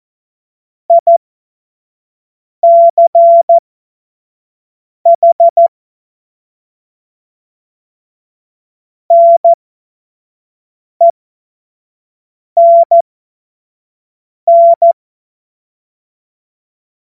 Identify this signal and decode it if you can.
Morse code, heard as a keyed tone, about 14 wpm, spaced as for 5 wpm, 680 Hz. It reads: ICH NENN
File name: Morsekurs / Witze / Witz 31.Oktober 2023 25 F70